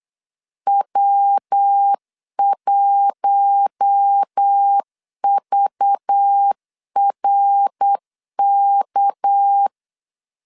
Morse code, heard as a keyed tone, about 8 wpm, 800 Hz. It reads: W1VRK
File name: LEZIONI / 6 PKX0 / 6 PKX 8wpm 3041